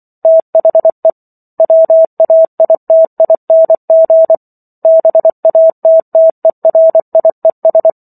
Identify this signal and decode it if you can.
Morse code, heard as a keyed tone, about 24 wpm, 650 Hz. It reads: THE WAITING BATTERIES